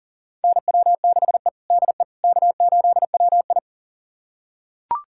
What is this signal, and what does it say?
Morse code, heard as a keyed tone, about 40 wpm, 700 Hz. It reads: NW6E DE K8WI
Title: USA Callsign DE Callsign - 4 Characters - 40wpm